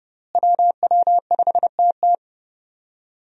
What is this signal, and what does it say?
Morse code, heard as a keyed tone, about 31 wpm, 700 Hz. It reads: WW5TT
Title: USA Call Signs - 5 Characters - Repeated 3x - 30wpm